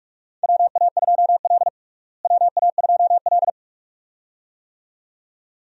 Morse code, heard as a keyed tone, about 46 wpm, 700 Hz. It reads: WA2L WA2L
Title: USA Call Signs - 4 Characters - Repeated 2x - 45wpm